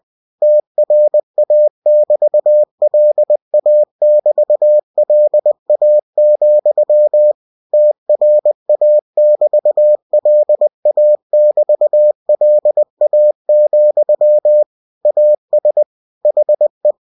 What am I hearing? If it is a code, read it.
Morse code, heard as a keyed tone, about 20 wpm, 600 Hz. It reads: TRA<BT>LA<BT>LA, TRA<BT>LA<BT>LA, AS HE